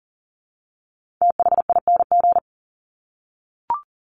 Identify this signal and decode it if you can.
Morse code, heard as a keyed tone, about 40 wpm, 700 Hz. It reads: THING
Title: Top 300 Words 40wpm